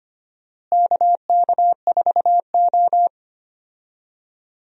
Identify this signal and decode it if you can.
Morse code, heard as a keyed tone, about 25 wpm, 700 Hz. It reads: KK4O
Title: USA Call Signs - 4 Characters - Repeated 2x - 25wpm